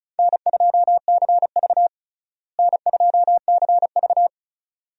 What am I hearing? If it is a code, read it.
Morse code, heard as a keyed tone, about 36 wpm, 700 Hz. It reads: N2CV N2CV
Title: USA Call Signs - 4 Characters - Repeated 2x - 35wpm